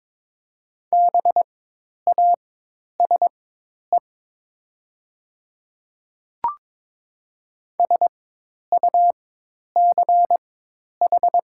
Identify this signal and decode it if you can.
Morse code, heard as a keyed tone, about 22 wpm, spaced as for 10 wpm, 700 Hz. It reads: BASE SUCH
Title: 4x Character Spacing - Top 500 Words - 22wpm